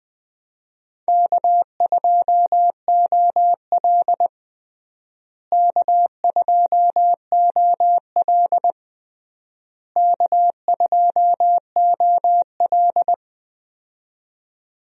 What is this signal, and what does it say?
Morse code, heard as a keyed tone, about 20 wpm, 700 Hz. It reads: K2OL K2OL K2OL
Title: USA Call Signs - 4 Characters - Repeated 3x - 20wpm